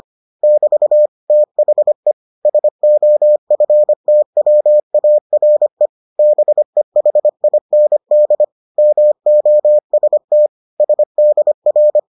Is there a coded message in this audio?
Yes: Morse code from 0.4 to 12.0 s, heard as a keyed tone, about 25 wpm, 600 Hz. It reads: <BT> THE SOFTWARE BEHIND MOST SDR